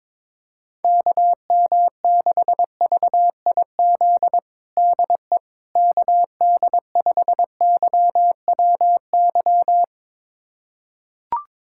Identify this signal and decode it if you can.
Morse code, heard as a keyed tone, about 22 wpm, 700 Hz. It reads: KM6VIZ DE KD5YWY
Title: USA Callsign DE Callsign - 6 Characters - 22wpm